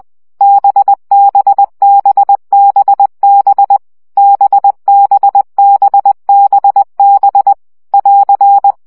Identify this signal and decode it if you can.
Morse code, heard as a keyed tone, about 20 wpm, 800 Hz. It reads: BBBBB BBBBB <AR>
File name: CW gyakorlas b 17 baud